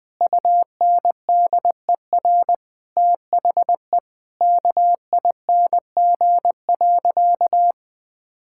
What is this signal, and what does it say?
Morse code, heard as a keyed tone, about 20 wpm, 700 Hz. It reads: UNDER THE KING.
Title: Sentences from Top 1000 Words - Review Part 6 - 20wpm